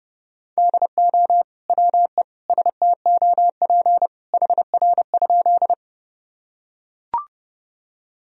Morse code, heard as a keyed tone, about 29 wpm, 700 Hz. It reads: DO WE STOP HR?
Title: Sentences from Top 300 Words and Abbreviations 30wpm